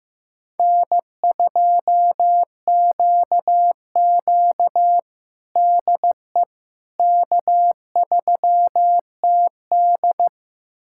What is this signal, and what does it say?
Morse code, heard as a keyed tone, about 15 wpm, 700 Hz. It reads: N2QQ DE K3TD
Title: USA Callsign DE Callsign - 4 Characters - 15wpm